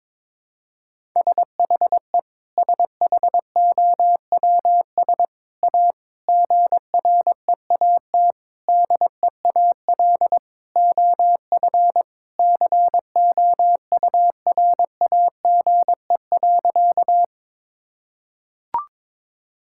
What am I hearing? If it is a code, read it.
Morse code, heard as a keyed tone, about 22 wpm, 700 Hz. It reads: SHE SHOWS A GREAT DEAL OF COURAGE.
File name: practice / 22wpm / Sentences with 4-Word Phrases - 22wpm